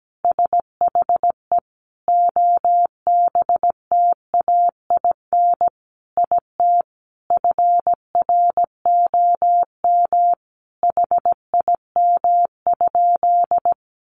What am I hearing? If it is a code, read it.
Morse code, heard as a keyed tone, about 17 wpm, 700 Hz. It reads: SHE OBTAIN IT FROM HIM?
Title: Sentences from Top 1300 Words 17wpm